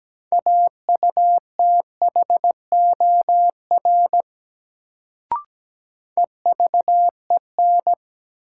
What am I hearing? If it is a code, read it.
Morse code, heard as a keyed tone, about 17 wpm, 700 Hz. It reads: AUTHOR EVEN
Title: Sets of 4 Words - New Word Drill - 17wpm